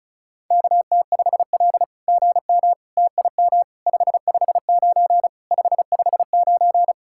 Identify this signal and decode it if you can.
Morse code, heard as a keyed tone, about 33 wpm, 700 Hz. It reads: KT5L GM TIM 559 559
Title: SOTA QSO Practice 35wpm